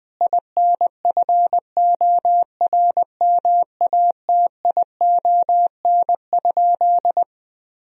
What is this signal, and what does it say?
Morse code, heard as a keyed tone, about 20 wpm, 700 Hz. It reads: INFORMATION?
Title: Sentences from Top 1000 Words - Review Part 4 - 20wpm